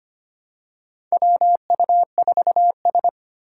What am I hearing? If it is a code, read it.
Morse code, heard as a keyed tone, about 25 wpm, 700 Hz. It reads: WU4S
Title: USA Call Signs - 4 Characters - 4x Character Spacing - 25wpm